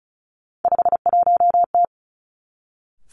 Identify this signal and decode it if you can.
Morse code, heard as a keyed tone, about 36 wpm, 700 Hz. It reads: 51T
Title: Three Challenging Letter-Number Combinations 35wpm